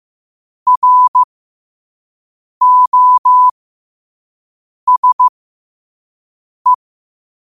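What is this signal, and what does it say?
Morse code, heard as a keyed tone, about 15 wpm, spaced as for 5 wpm, 1000 Hz. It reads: ROSE
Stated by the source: Morse quote of the day at 5 wpm.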